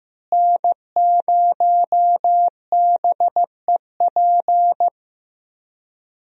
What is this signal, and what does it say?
Morse code, heard as a keyed tone, about 15 wpm, 700 Hz. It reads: N0BEP
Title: USA Call Signs - 5 Characters - Repeated 3x - 15wpm